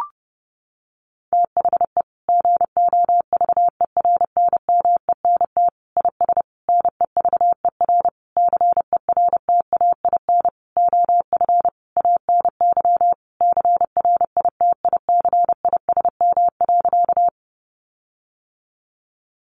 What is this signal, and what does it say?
Morse code, heard as a keyed tone, about 30 wpm, 700 Hz. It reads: THE GOVERNMENT IS NEVER CERTAIN OF ANY CRITICISM.
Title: Sentences from Top 1300 Words 30wpm